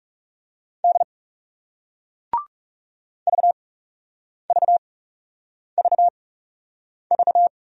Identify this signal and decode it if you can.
Morse code, heard as a keyed tone, about 39 wpm, spaced as for 8 wpm, 700 Hz. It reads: D VVVV